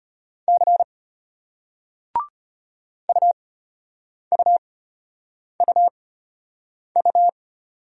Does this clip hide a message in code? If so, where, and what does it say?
Yes: Morse code from 0.5 to 7.3 s, heard as a keyed tone, about 34 wpm, spaced as for 8 wpm, 700 Hz. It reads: C UUUU